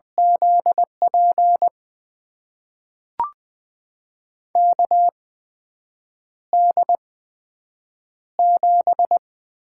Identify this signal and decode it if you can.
Morse code, heard as a keyed tone, about 20 wpm, spaced as for 7 wpm, 700 Hz. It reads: ZP KD7
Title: USA Call Signs - 6 Characters - 8x Character Spacing - 20wpm